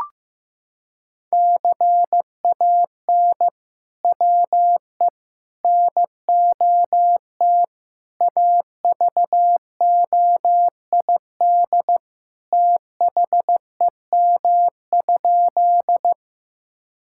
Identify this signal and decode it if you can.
Morse code, heard as a keyed tone, about 15 wpm, 700 Hz. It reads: CAN WE NOT AVOID THEM?
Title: Sentences from Top 1000 Words - Review Part 6 - 15wpm